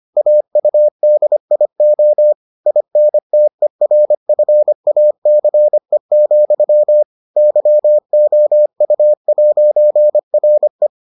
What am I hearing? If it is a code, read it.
Morse code, heard as a keyed tone, about 25 wpm, 600 Hz. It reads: AUDIO INTERFACE, YOU'RE